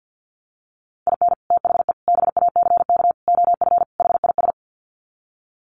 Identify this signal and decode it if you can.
Morse code, heard as a keyed tone, about 50 wpm, 700 Hz. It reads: IN THE BACK OF HIS
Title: Sets of 5 Words Encore2 50wpm